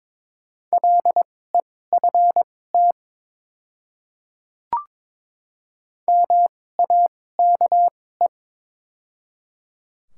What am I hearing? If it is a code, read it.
Morse code, heard as a keyed tone, about 22 wpm, spaced as for 16 wpm, 700 Hz. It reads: LEFT MAKE
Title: 2x Character Spacing - Top 300 Words - 22wpm